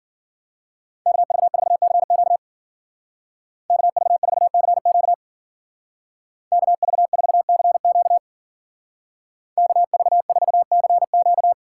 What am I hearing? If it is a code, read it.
Morse code, heard as a keyed tone, about 51 wpm, 700 Hz. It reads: KV4CQ KV4CQ KV4CQ KV4CQ